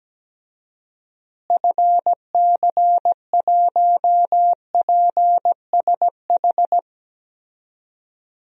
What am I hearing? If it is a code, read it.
Morse code, heard as a keyed tone, about 17 wpm, 700 Hz. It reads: FC1PSH
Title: DXCC-100 Call Signs - 6 Characters - Repeated 2x - 17wpm